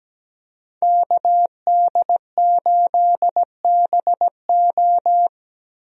Morse code, heard as a keyed tone, about 17 wpm, 700 Hz. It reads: KD8BO
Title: USA Call Signs - 5 Characters - 17wpm